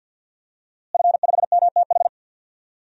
Morse code, heard as a keyed tone, about 46 wpm, 700 Hz. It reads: F5MTH
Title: DXCC-100 Call Signs - 5 Characters - Repeated 2x - 50wpm